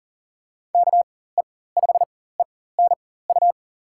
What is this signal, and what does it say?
Morse code, heard as a keyed tone, about 40 wpm, spaced as for 19 wpm, 700 Hz. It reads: KE5ENU
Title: USA Call Signs - 6 Characters - 4x Character Spacing - 40wpm